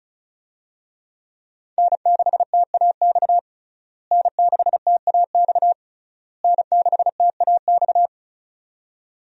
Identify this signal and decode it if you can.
Morse code, heard as a keyed tone, about 36 wpm, 700 Hz. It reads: N6TAX N6TAX N6TAX
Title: USA Call Signs - 5 Characters - Repeated 3x - 35wpm